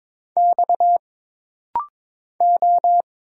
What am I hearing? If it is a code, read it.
Morse code, heard as a keyed tone, about 22 wpm, 700 Hz. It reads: X O
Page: This rapid-fire timing is geared to push your ICR (Instant Character Recognition) skills.